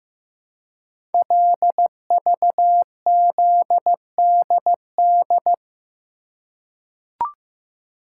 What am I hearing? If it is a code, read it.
Morse code, heard as a keyed tone, about 15 wpm, 700 Hz. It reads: LVZDD
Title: Code Groups - 5 Letters - 15wpm